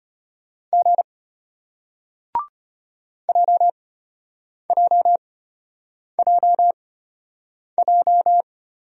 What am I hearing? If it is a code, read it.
Morse code, heard as a keyed tone, about 33 wpm, spaced as for 8 wpm, 700 Hz. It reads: G JJJJ